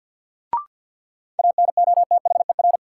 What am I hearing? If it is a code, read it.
Morse code, heard as a keyed tone, about 52 wpm, 700 Hz. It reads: ANOTHER
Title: Sets of 5 Words Encore - New Word Drill - 50wpm